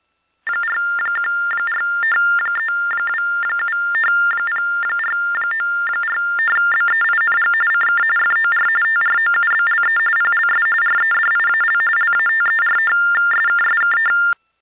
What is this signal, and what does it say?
TTY/TDD text telephone tones, about 45.45 baud. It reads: 4143270494BEING CK FOR TROUBLE PLS REDIAL LATERSKSK